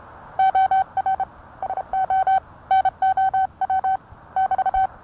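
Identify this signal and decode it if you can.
Morse code, heard as a keyed tone, about 30 wpm, 760 Hz. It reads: OR SO NOW <BT>
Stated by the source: Morse code sent by N1EA on a Vibroplex key.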